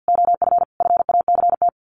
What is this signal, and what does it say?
Morse code, heard as a keyed tone, about 49 wpm, 700 Hz. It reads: OF FACT
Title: Sets of 4 Words 50wpm